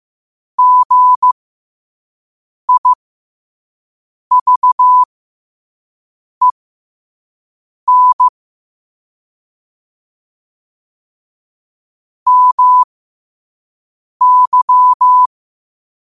Morse code, heard as a keyed tone, about 15 wpm, spaced as for 5 wpm, 1000 Hz. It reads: GIVEN MY